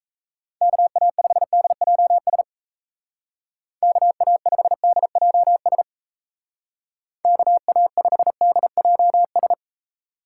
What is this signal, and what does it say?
Morse code, heard as a keyed tone, about 35 wpm, 700 Hz. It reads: KA5DJS KA5DJS KA5DJS
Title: Speed Racing - USA 2x3 Call Signs 28wpm